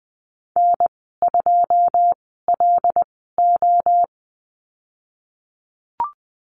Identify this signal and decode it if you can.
Morse code, heard as a keyed tone, about 20 wpm, spaced as for 15 wpm, 700 Hz. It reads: N2LO
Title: 167 Lesson 27g - Number 7 - Call Signs 20wpm